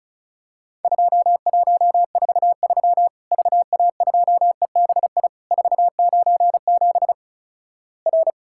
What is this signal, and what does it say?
Morse code, heard as a keyed tone, about 36 wpm, 700 Hz. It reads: 2143 VA2EBI 497 R
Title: Worked All Europe DX Contest - QTC Exchange - 35wpm